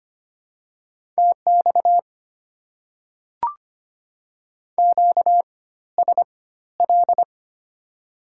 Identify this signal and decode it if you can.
Morse code, heard as a keyed tone, about 25 wpm, spaced as for 14 wpm, 690 Hz. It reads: TX QSL
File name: Top 90 QSO Elements - 4x Character Spacing - 25wpm